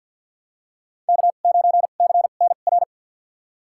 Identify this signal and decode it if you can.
Morse code, heard as a keyed tone, about 53 wpm, spaced as for 37 wpm, 700 Hz. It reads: K9CNR
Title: USA Call Signs - 5 Characters - 2x Character Spacing - 50wpm